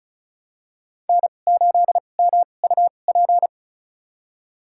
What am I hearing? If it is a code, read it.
Morse code, heard as a keyed tone, about 36 wpm, spaced as for 25 wpm, 700 Hz. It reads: N8MUP
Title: USA Call Signs - 5 Characters - 2x Character Spacing - 35wpm